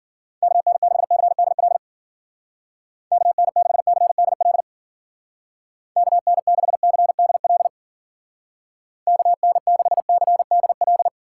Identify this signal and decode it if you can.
Morse code, heard as a keyed tone, about 49 wpm, 700 Hz. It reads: KN6CDL KN6CDL KN6CDL KN6CDL
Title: Speed Racing - USA 2x3 Call Signs 40wpm